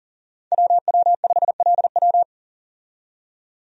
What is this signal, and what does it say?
Morse code, heard as a keyed tone, about 38 wpm, 700 Hz. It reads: WW5LW
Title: USA Call Signs - 5 Characters - 40wpm